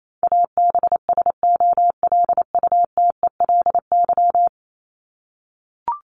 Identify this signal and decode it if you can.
Morse code, heard as a keyed tone, about 28 wpm, 700 Hz. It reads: ABSOLUTELY
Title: LY Suffix - Top 100 Words with Base Words 28wpm